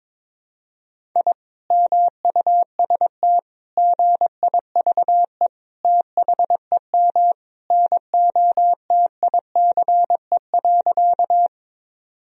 Morse code, heard as a keyed tone, about 22 wpm, 700 Hz. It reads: I MUST GIVE THEM NOTICE.
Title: Sentences from Top 1000 Words - Review Part 2 - 22wpm